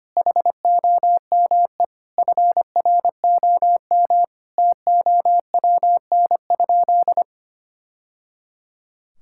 Morse code, heard as a keyed tone, about 25 wpm, 700 Hz. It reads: HOME FROM TOWN?
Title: Sentences from Top 1000 Words - Review Part 5 - 25wpm